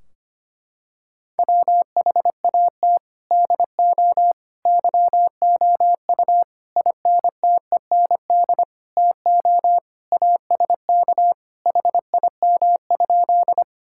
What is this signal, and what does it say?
Morse code, heard as a keyed tone, about 25 wpm, 700 Hz. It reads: WHAT DO YOU INTEND TO ASK HIM?
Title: Sentences from Top 1600 Words 25wpm